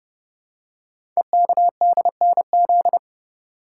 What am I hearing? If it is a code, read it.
Morse code, heard as a keyed tone, about 30 wpm, 690 Hz. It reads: EKDNZ
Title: Code Groups - 5 Letters - 30wpm